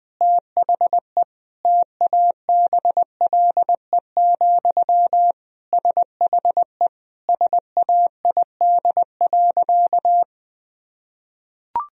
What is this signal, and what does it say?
Morse code, heard as a keyed tone, about 21 wpm, 700 Hz. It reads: THE TABLE, SHE SAID.
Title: Sentences from Top 1000 Words - Review Part 2 - 20wpm